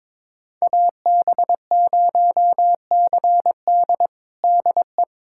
Morse code, heard as a keyed tone, about 21 wpm, 700 Hz. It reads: AB0CD DE
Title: USA Callsign DE Callsign 22wpm